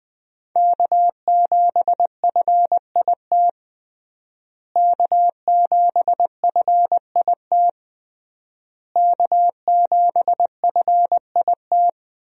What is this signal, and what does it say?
Morse code, heard as a keyed tone, about 20 wpm, 700 Hz. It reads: K7FIT K7FIT K7FIT
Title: USA Call Signs - 5 Characters - Repeated 3x - 20wpm